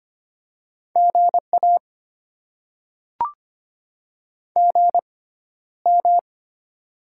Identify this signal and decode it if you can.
Morse code, heard as a keyed tone, about 25 wpm, spaced as for 13 wpm, 700 Hz. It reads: GA GM
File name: Top 20 QSO Elements - 6x Character Spacing - 25wpm